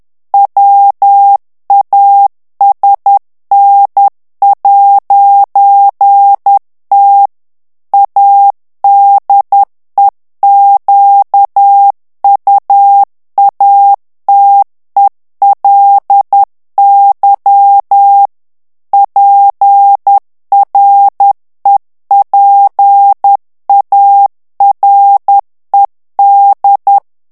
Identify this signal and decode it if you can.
Morse code, heard as a keyed tone, about 10 wpm, 800 Hz. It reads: WASN'T ADEQUATELY PREPARED
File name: ANGLAIS texte clair 2 vitesse 10 mots minute